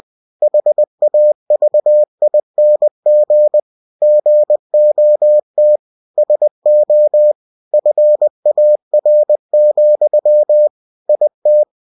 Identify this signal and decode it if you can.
Morse code, heard as a keyed tone, about 20 wpm, 600 Hz. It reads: HAVING GOT SO FAR, IT